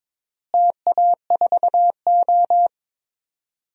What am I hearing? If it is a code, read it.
Morse code, heard as a keyed tone, about 22 wpm, 700 Hz. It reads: TA4O
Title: DXCC-100 Call Signs - 4 Characters - Repeated 3x - 22wpm